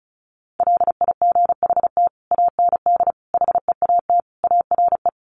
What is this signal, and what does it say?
Morse code, heard as a keyed tone, about 34 wpm, 700 Hz. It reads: LIGHT AND HEAT ARE
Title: Sentences from Top 1300 Words 35wpm